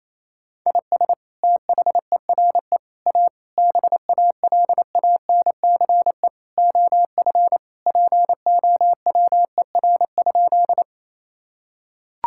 Morse code, heard as a keyed tone, about 27 wpm, 700 Hz. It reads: IS THERE A BALANCE OF POWER?
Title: Sentences from Top 1000 Words - Review Part 6 - 28wpm